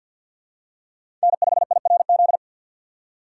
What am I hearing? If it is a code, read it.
Morse code, heard as a keyed tone, about 51 wpm, 700 Hz. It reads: N5IRZ
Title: USA Call Signs - 5 Characters - 2x Character Spacing - 50wpm